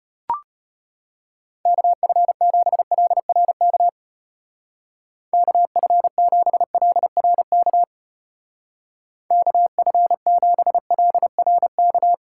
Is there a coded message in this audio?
Yes: Morse code from 1.7 to 12.1 s, heard as a keyed tone, about 32 wpm, 700 Hz. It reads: KF7LRK KF7LRK KF7LRK